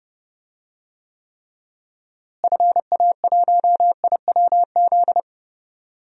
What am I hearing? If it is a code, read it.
Morse code, heard as a keyed tone, about 31 wpm, 700 Hz. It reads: FA1IWZ
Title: DXCC-100 Call Signs - 6 Characters - Repeated 3x - 30wpm